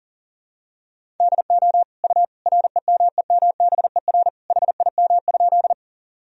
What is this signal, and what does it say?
Morse code, heard as a keyed tone, about 37 wpm, 700 Hz. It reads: DO U REMEMBER HIM?
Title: Sentences from Top 400 Words and Abbreviations 40wpm